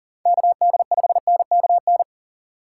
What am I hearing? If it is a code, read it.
Morse code, heard as a keyed tone, about 39 wpm, 700 Hz. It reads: KD5NKN
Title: USA Call Signs - 3x Character Spacing - 40wpm